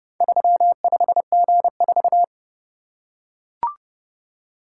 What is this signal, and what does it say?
Morse code, heard as a keyed tone, about 29 wpm, 700 Hz. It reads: 35G4